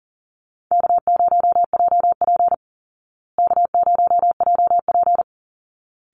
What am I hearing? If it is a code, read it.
Morse code, heard as a keyed tone, about 37 wpm, 700 Hz. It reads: K0JP K0JP
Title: Call Signs Repeated 3x - 40wpm